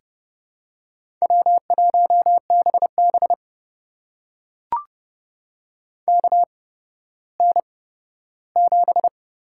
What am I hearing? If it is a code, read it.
Morse code, heard as a keyed tone, about 30 wpm, spaced as for 15 wpm, 700 Hz. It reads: W1BB KN7
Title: USA Call Signs - 4 Characters - 8x Character Spacing - 30wpm